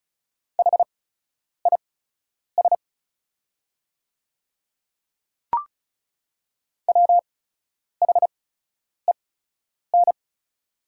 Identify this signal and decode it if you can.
Morse code, heard as a keyed tone, about 34 wpm, spaced as for 10 wpm, 700 Hz. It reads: HIS WHEN
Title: 8x Character Spacing - Top 100 Words - 35wpm